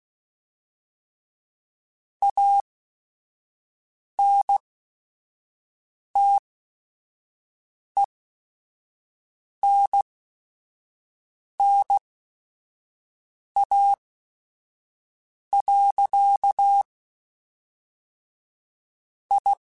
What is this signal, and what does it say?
Morse code, heard as a keyed tone, about 16 wpm, spaced as for 5 wpm, 800 Hz. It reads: ANTENNA. I
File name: QSOs / QSOs-mp3s / kc7sac@5